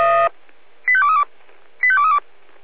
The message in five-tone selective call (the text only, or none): 85232 85232